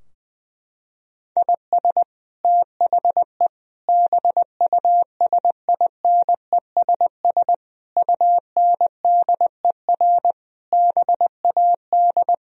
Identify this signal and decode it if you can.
Morse code, heard as a keyed tone, about 20 wpm, 700 Hz. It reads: IS THE BUSINESS UNDER BAD
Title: Sentences from Top 1000 Words - Review Part 4 - 20wpm